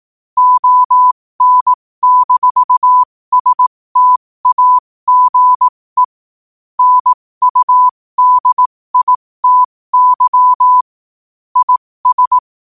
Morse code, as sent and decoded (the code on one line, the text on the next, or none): --- -. -....- ... - .- --. / -. ..- -.. .. - -.-- / .. ...
ON-STAGE NUDITY IS